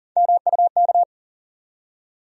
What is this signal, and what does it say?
Morse code, heard as a keyed tone, about 40 wpm, 700 Hz. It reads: MUK
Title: USA Call Sign Postfixes - 3 Letters 40wpm